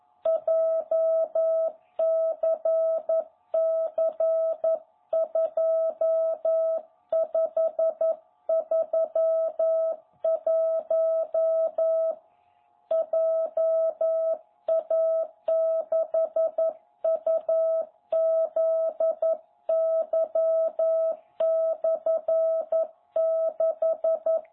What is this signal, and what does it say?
Morse code, heard as a keyed tone, about 11 wpm, 660 Hz. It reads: JCC2531 JA6UZY/6
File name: jh8cbh / 2021cw